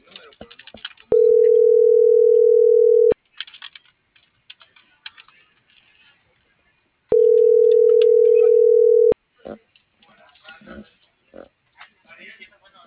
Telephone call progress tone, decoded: ringback tone

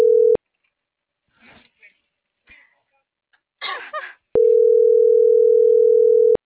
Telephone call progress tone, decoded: ringback tone

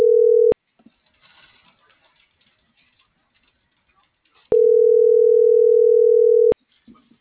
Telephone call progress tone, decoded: ringback tone

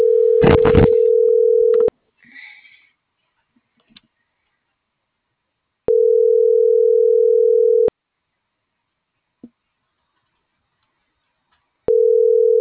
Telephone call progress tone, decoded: ringback tone

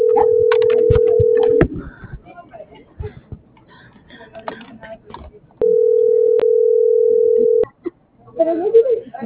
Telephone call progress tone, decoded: ringback tone

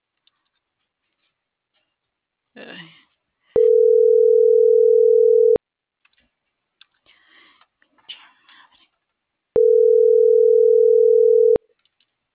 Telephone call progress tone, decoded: ringback tone

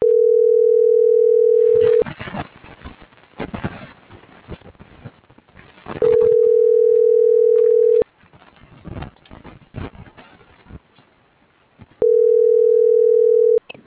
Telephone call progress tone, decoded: ringback tone

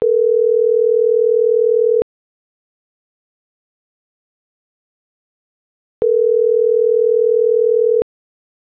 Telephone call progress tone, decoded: ringback tone

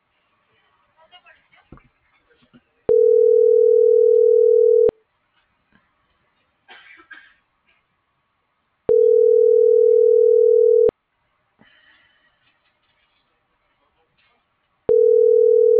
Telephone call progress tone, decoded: ringback tone

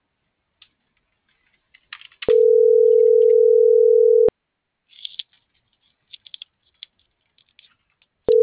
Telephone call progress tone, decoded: ringback tone

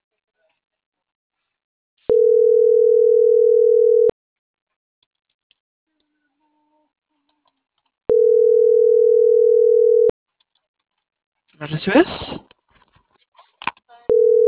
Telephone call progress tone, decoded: ringback tone

